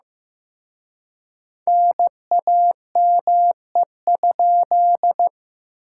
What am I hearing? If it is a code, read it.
Morse code, heard as a keyed tone, about 18 wpm, 700 Hz. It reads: NAME?
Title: Speed-Racing Common QSO Phrases Encore - 15wpm